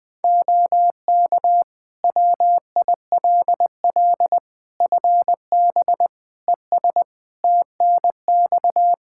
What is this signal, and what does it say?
Morse code, heard as a keyed tone, about 21 wpm, 700 Hz. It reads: OK WILL FB ES TNX
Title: Practice QSO - Exchange 2 - Info - 20wpm